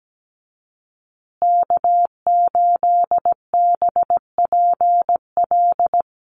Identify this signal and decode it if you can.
Morse code, heard as a keyed tone, about 17 wpm, 700 Hz. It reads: K8BPL
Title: USA Call Signs Repeated 2x 17wpm